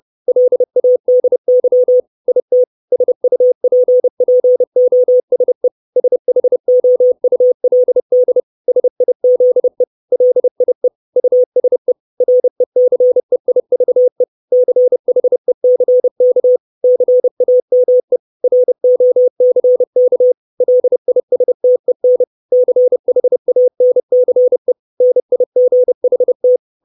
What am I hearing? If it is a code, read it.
Morse code, heard as a keyed tone, about 30 wpm, 500 Hz. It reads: LADY IT SUPPOSE SHOULD SIZE LIE USE RECEIVE CHECK CAME ROCK LISTEN CHANCE NIGHT
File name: 632 Words 30WPM